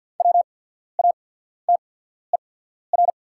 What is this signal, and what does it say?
Morse code, heard as a keyed tone, about 48 wpm, spaced as for 14 wpm, 700 Hz. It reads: WATER
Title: 8x Character Spacing - Top 500 Words - 50wpm